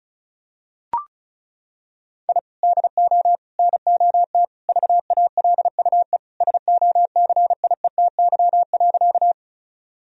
Morse code, heard as a keyed tone, about 36 wpm, 700 Hz. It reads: I DO NOT VALUE SOCIETY.
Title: Sentences from Top 1000 Words - Review Part 4 - 35wpm